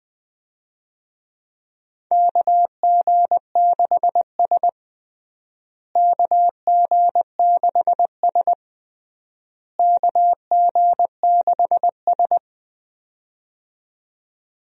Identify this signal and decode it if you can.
Morse code, heard as a keyed tone, about 20 wpm, 700 Hz. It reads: KG6S KG6S KG6S
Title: USA Call Signs - 4 Characters - Repeated 3x - 20wpm